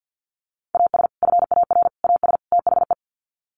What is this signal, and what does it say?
Morse code, heard as a keyed tone, about 50 wpm, 700 Hz. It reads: AS FAR AS THE
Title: Sets of 4 Words 50wpm